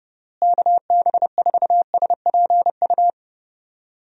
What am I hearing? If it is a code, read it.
Morse code, heard as a keyed tone, about 30 wpm, 700 Hz. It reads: KB4SPU
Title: USA Call Signs - 6 Characters - Repeated 3x - 30wpm